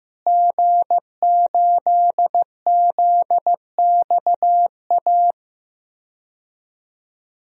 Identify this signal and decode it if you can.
Morse code, heard as a keyed tone, about 15 wpm, 700 Hz. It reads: G8ZXA
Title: DXCC-100 Call Signs - 5 Characters - Repeated 3x - 15wpm